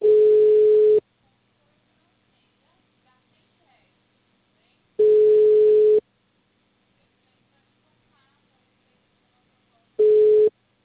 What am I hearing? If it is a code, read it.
Ringback tone (Europe, 425 Hz), with speech.